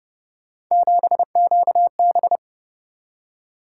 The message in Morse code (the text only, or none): 7QB